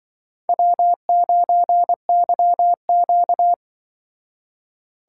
Morse code, heard as a keyed tone, about 24 wpm, 700 Hz. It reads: W9YQ
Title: Speed Racing - USA 1x2 Call Signs 20wpm